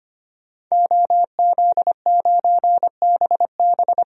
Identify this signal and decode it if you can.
Morse code, heard as a keyed tone, about 25 wpm, 700 Hz. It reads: OZ9BB
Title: DXCC-100 Call Signs - 5 Characters - 25wpm